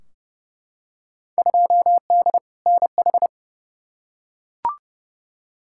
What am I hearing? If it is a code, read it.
Morse code, heard as a keyed tone, about 30 wpm, 700 Hz. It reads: 2D NH
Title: Field Day Exchanges Encore - 30wpm